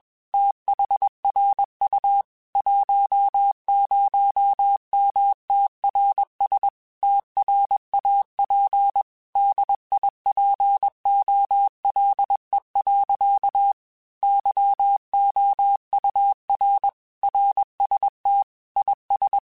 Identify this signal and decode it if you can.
Morse code, heard as a keyed tone, about 21 wpm, 800 Hz. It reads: THRU 10MTRS TRAP DIPOLE. YOUR RST IS
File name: QSOs / QSOs-mp3s / WD0FBT 1@20